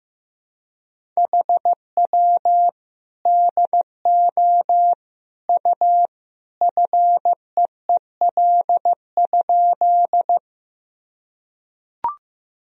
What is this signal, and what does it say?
Morse code, heard as a keyed tone, about 16 wpm, 700 Hz. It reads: HW DO U FEEL?
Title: Sentences from Top 400 Words and Abbreviations 15wpm